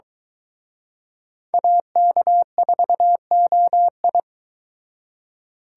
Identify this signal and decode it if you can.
Morse code, heard as a keyed tone, about 24 wpm, 700 Hz. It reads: AK4OI